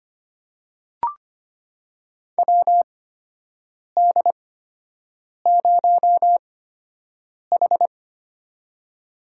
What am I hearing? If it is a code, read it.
Morse code, heard as a keyed tone, about 25 wpm, spaced as for 7 wpm, 700 Hz. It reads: WD0H
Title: USA Call Signs - 4 Characters - 8x Character Spacing - 25wpm